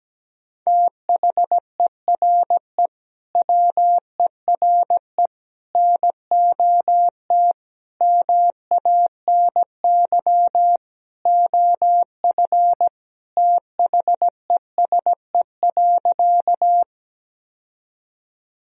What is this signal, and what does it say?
Morse code, heard as a keyed tone, about 17 wpm, 700 Hz. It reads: THERE WERE NOT MANY OF THESE.
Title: Sentences from Top 1000 Words - Review Part 1 - 17wpm